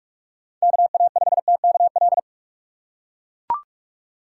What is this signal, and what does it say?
Morse code, heard as a keyed tone, about 43 wpm, 700 Hz. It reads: KA5TKL
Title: USA Call Signs - 6 Characters - 3x Character Spacing - 45wpm